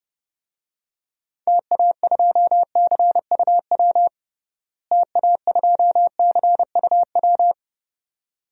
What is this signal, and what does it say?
Morse code, heard as a keyed tone, about 30 wpm, 700 Hz. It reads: TA2CUW TA2CUW
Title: DXCC-100 Call Signs - 6 Characters - Repeated 2x - 30wpm